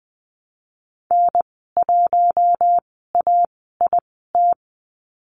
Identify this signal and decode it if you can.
Morse code, heard as a keyed tone, about 20 wpm, spaced as for 14 wpm, 700 Hz. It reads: N1AIT
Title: 082 Lesson 14g - Number 2 - Call Signs 20wpm